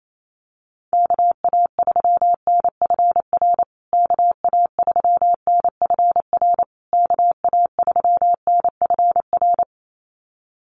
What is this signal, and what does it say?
Morse code, heard as a keyed tone, about 28 wpm, 700 Hz. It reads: KA3NFR KA3NFR KA3NFR
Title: USA Call Signs Repeated 3x 28wpm